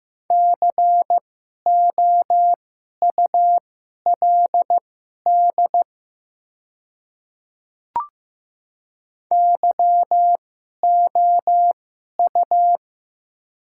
Morse code, heard as a keyed tone, about 15 wpm, spaced as for 11 wpm, 700 Hz. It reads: COULD YOU
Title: 2x Character Spacing - Top 100 Words - 15wpm